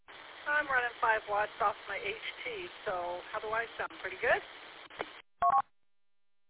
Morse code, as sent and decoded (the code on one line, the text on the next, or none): none